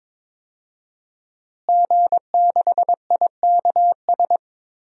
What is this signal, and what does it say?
Morse code, heard as a keyed tone, about 22 wpm, 700 Hz. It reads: G6IKS
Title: DXCC-100 Call Signs - 5 Characters - Repeated 2x - 22wpm